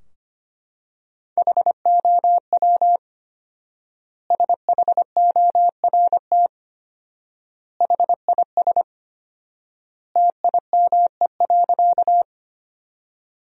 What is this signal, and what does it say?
Morse code, heard as a keyed tone, about 25 wpm, 700 Hz. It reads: HOW SHORT HIS TIME.
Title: 4x Word-Spacing - Sentences from Top 400 Words - 25wpm